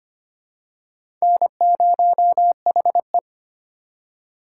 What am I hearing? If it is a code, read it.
Morse code, heard as a keyed tone, about 25 wpm, 700 Hz. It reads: N0HE
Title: USA Call Signs - 2x Character Spacing - 25wpm